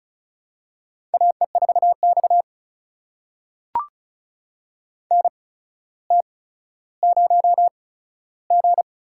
Morse code, heard as a keyed tone, about 36 wpm, spaced as for 15 wpm, 700 Hz. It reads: AE4X NT0G
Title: USA Call Signs - 4 Characters - 8x Character Spacing - 35wpm